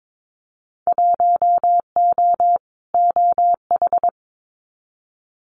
Morse code, heard as a keyed tone, about 22 wpm, 700 Hz. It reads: 1O OH
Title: Winter Field Day Exchanges 22wpm